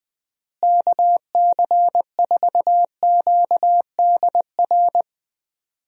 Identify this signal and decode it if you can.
Morse code, heard as a keyed tone, about 20 wpm, 700 Hz. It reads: KC4QDR
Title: USA Call Signs - 6 Characters - Repeated 2x - 20wpm